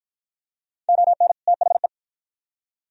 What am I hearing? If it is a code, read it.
Morse code, heard as a keyed tone, about 52 wpm, 700 Hz. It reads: ON THE